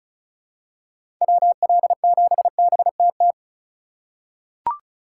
Morse code, heard as a keyed tone, about 34 wpm, 700 Hz. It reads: WL7BTT
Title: USA Call Signs - 6 Characters - 2x Character Spacing - 35wpm